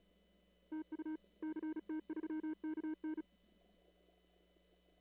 Morse code, heard as a keyed tone, about 36 wpm, 310 Hz. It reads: TU CT3KN
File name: CW / kt5j wpx cw-op-k5pi